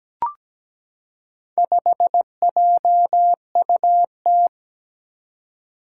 Morse code, heard as a keyed tone, about 17 wpm, 700 Hz. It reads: 5JUT
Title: Four Challenging Letter-Number Combinations 17wpm